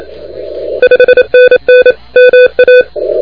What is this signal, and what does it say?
Morse code, heard as a keyed tone, about 27 wpm, 2500 Hz, with noise. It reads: 5NN MA
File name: recordings / HG7T de NT1K ARRL10M CW 28MHz